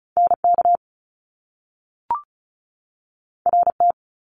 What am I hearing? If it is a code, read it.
Morse code, heard as a keyed tone, about 36 wpm, 700 Hz. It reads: NK RT
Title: Top Two Letter Combinations in Words 35wpm